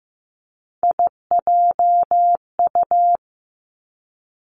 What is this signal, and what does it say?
Morse code, heard as a keyed tone, about 15 wpm, 700 Hz. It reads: IJU
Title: Three Challenging Letter-Number Combinations 15wpm